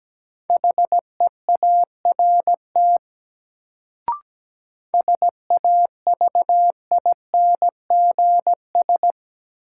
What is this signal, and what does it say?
Morse code, heard as a keyed tone, about 17 wpm, 700 Hz. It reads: HEART SAVINGS
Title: Sentences from Top 1200 Words - New Word Drill Alt - 17wpm